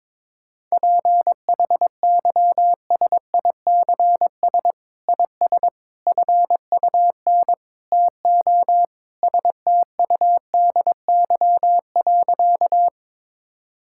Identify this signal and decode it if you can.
Morse code, heard as a keyed tone, about 22 wpm, 700 Hz. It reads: PHYSICS IS FUN TO STUDY.
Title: Sentences from Top 1000 Words - Review Part 5 - 22wpm